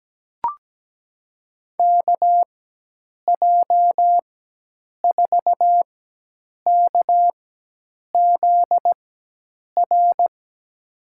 Morse code, heard as a keyed tone, about 17 wpm, spaced as for 8 wpm, 700 Hz. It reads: KJ4KZR
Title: USA Call Signs - 6 Characters - 4x Character Spacing - 17wpm